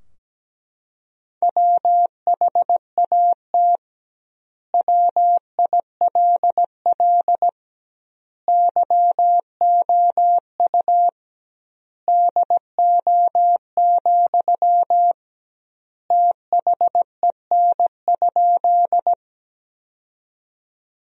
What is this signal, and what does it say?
Morse code, heard as a keyed tone, about 17 wpm, 700 Hz. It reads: WHAT WILL YOU DO, THEN?
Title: 2x Word-Spacing - Sentences from Top 100 Words - 17wpm